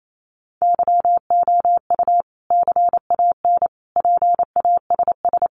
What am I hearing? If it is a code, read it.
Morse code, heard as a keyed tone, about 28 wpm, 700 Hz. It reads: YOU CAN PASS
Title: Sentences from Top 400 Words 28wpm